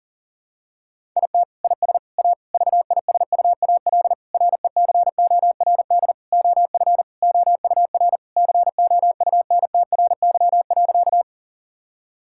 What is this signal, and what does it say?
Morse code, heard as a keyed tone, about 40 wpm, 700 Hz. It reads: IT IS A VISUAL RECORD OF OUR COUNTRY.
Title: Sentences from Top 1000 Words - Review Part 6 - 40wpm